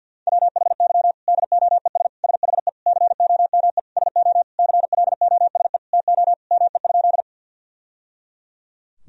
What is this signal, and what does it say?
Morse code, heard as a keyed tone, about 47 wpm, 700 Hz. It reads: WHY DOES SHE COME SO CLOSE TO ME?
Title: Sentences from Top 1000 Words - Review Part 1 - 50wpm